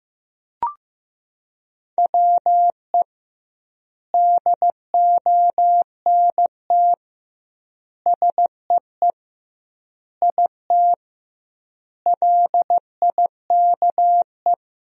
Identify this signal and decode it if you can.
Morse code, heard as a keyed tone, about 16 wpm, 700 Hz. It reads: WE DONT SEE IT LIKE
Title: 2x Word-Spacing - Sentences from Top 300 Words - 15wpm